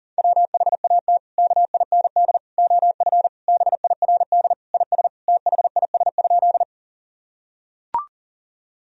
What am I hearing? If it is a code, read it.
Morse code, heard as a keyed tone, about 37 wpm, 700 Hz. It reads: WHAT KIND OF BIRD IS THIS?
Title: Sentences from Top 1000 Words - Review Part 2 - 40wpm